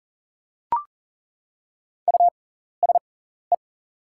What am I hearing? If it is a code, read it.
Morse code, heard as a keyed tone, about 38 wpm, spaced as for 14 wpm, 690 Hz. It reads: USE
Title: 6x Character Spacing - Top 500 Words - 40wpm